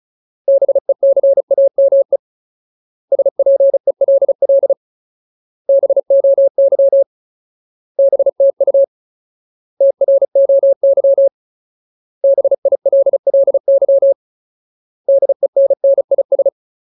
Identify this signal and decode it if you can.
Morse code, heard as a keyed tone, about 36 wpm, 550 Hz. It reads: BECAME SPELL BOY BTU TROY BILLY DENNIS
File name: Ultimate Word List / 35WPM Ultimate